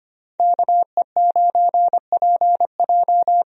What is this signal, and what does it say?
Morse code, heard as a keyed tone, about 25 wpm, 700 Hz. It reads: KE9PJ
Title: USA Call Signs - 5 Characters - Repeated 2x - 25wpm